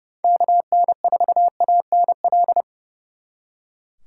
Morse code, heard as a keyed tone, about 39 wpm, 700 Hz. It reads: KN4ANL